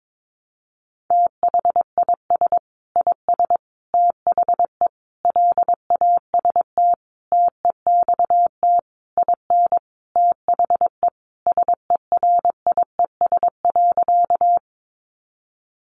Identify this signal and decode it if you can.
Morse code, heard as a keyed tone, about 22 wpm, 700 Hz. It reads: THIS IS THE LAST TEXT IN THE SERIES.
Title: Sentences from Top 1200 Words 22wpm